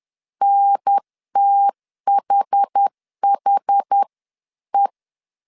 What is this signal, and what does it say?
Morse code, heard as a keyed tone, about 10 wpm, 800 Hz. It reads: NTHHE
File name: 10wpm